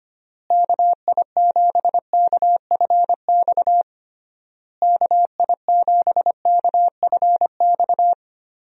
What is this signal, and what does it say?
Morse code, heard as a keyed tone, about 25 wpm, 700 Hz. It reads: KI7KFX KI7KFX
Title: USA Call Signs - 6 Characters - Repeated 3x - 25wpm